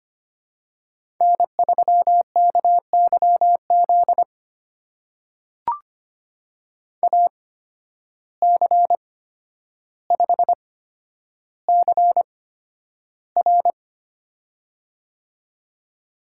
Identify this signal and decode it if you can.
Morse code, heard as a keyed tone, about 25 wpm, spaced as for 11 wpm, 700 Hz. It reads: N3KYZ AC5CR
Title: USA Call Signs - 5 Characters - 8x Character Spacing - 25wpm